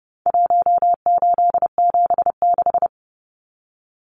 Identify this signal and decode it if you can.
Morse code, heard as a keyed tone, about 30 wpm, 700 Hz. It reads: 1876
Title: Numbers - Contest Style - 4-Digit 30wpm